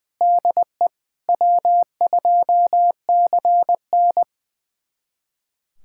Morse code, heard as a keyed tone, about 20 wpm, 700 Hz. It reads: DE W2CN
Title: USA Callsign DE Callsign 20wpm